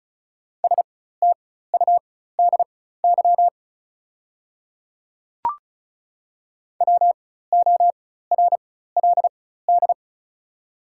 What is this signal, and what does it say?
Morse code, heard as a keyed tone, about 36 wpm, spaced as for 16 wpm, 700 Hz. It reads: STUDY WORLD
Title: 4x Character Spacing - Top 300 Words - 35wpm